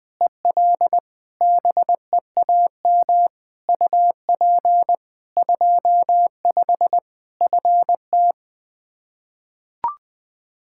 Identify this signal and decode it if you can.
Morse code, heard as a keyed tone, about 20 wpm, 700 Hz. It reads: EL BEAM UP 25 FT
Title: Practice QSO - Exchange 2 - Info - 20wpm